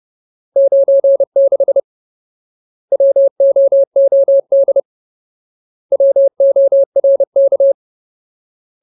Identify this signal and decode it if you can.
Morse code, heard as a keyed tone, about 30 wpm, 550 Hz. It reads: 96 WOOD WORK
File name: Ultimate Word List / 30WPM Ultimate